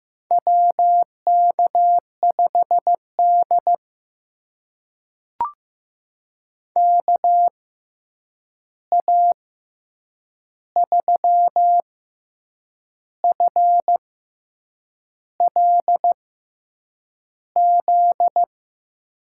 Morse code, heard as a keyed tone, about 15 wpm, spaced as for 7 wpm, 700 Hz. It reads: WK5D KA3FLZ
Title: USA Call Signs - 6x Character Spacing - 15wpm